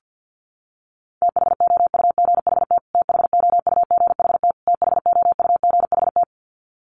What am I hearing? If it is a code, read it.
Morse code, heard as a keyed tone, about 50 wpm, 700 Hz. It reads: THOUGHT THOUGHT THOUGHT
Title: Top 500 Words Repeated 50wpm